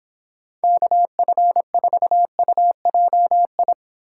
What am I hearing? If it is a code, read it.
Morse code, heard as a keyed tone, about 26 wpm, 700 Hz. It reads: KF4UJI